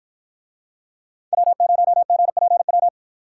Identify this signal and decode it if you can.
Morse code, heard as a keyed tone, about 49 wpm, 700 Hz. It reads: W0GPW